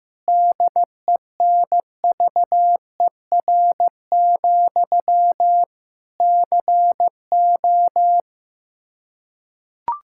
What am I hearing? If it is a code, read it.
Morse code, heard as a keyed tone, about 15 wpm, 700 Hz. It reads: DENVER, CO
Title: K1USN SST QSO Practice 15wpm